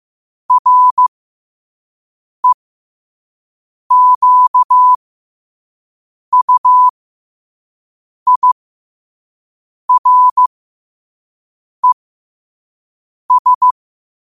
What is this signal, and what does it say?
Morse code, heard as a keyed tone, about 15 wpm, spaced as for 5 wpm, 1000 Hz. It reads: REQUIRES
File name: qotdfiles / 5wpm